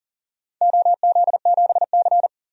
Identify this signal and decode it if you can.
Morse code, heard as a keyed tone, about 42 wpm, 700 Hz. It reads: OZ7C
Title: DXCC-100 Call Signs - 4 Characters - 40wpm